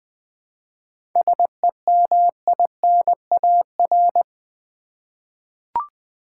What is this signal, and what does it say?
Morse code, heard as a keyed tone, about 21 wpm, 700 Hz. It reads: SEMINAR
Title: 7 Letter Words - First 3 Letters Given - 20wpm